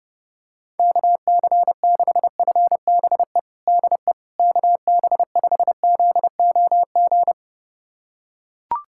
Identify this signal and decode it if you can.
Morse code, heard as a keyed tone, about 30 wpm, 700 Hz. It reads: KC6FBE DE KB5ZOG
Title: USA Callsign DE Callsign - 6 Characters - 30wpm